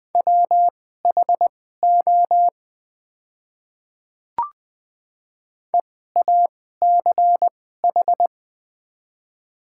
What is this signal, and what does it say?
Morse code, heard as a keyed tone, about 21 wpm, spaced as for 15 wpm, 700 Hz. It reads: WHO EACH
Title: 2x Character Spacing - Top 100 Words - 20wpm